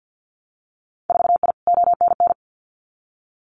Morse code, heard as a keyed tone, about 50 wpm, 700 Hz. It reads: 4I ONN
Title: Winter Field Day Exchanges Encore 50wpm